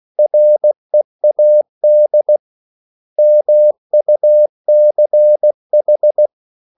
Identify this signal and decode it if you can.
Morse code, heard as a keyed tone, about 16 wpm, 600 Hz. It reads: READ MUCH
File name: CW / Practice / 500MostCommonEnglishWords16